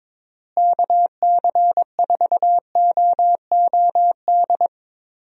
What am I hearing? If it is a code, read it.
Morse code, heard as a keyed tone, about 22 wpm, 700 Hz. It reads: KC4OOD
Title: USA Call Signs - 6 Characters - Repeated 3x - 22wpm